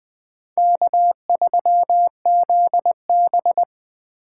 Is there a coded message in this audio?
Yes: Morse code from 0.6 to 3.6 s, heard as a keyed tone, about 20 wpm, 700 Hz. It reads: K3ZB